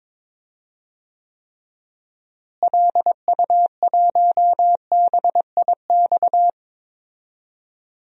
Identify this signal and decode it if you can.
Morse code, heard as a keyed tone, about 22 wpm, 700 Hz. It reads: LU1BIX
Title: DXCC-100 Call Signs - 6 Characters - Repeated 3x - 22wpm